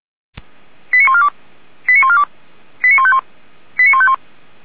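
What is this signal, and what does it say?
Five-tone selective call (ZVEI), heard as quick runs of five tones, tones of about 70 ms. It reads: 89142 89142 89151 89151
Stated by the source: Five-tone ZVEI selective call received by SDR, from the audio server of Feuerwehr Orlamünde (fire brigade).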